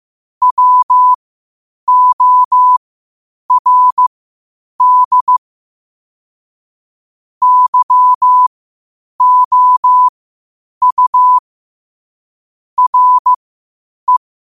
Morse code, heard as a keyed tone, about 15 wpm, spaced as for 8 wpm, 1000 Hz. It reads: WORD YOU RE
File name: 8wpm